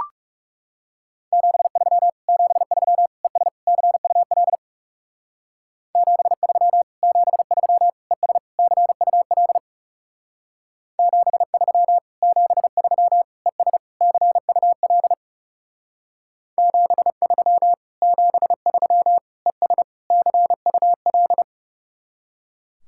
Morse code, heard as a keyed tone, about 36 wpm, 700 Hz. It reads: 73 73 ES CUL 73 73 ES CUL 73 73 ES CUL 73 73 ES CUL
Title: Speed-Racing Common QSO Phrases Encore - 30wpm